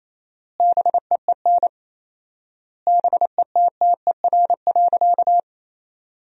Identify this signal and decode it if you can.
Morse code, heard as a keyed tone, about 28 wpm, 700 Hz. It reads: BEEN BETTER.
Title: 4x Word-Spacing - Sentences from Top 500 Words - 28wpm